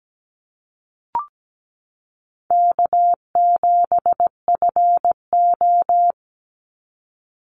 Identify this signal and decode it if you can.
Morse code, heard as a keyed tone, about 17 wpm, 700 Hz. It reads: K7FO
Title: NA SOTA Chaser Callsigns 17wpm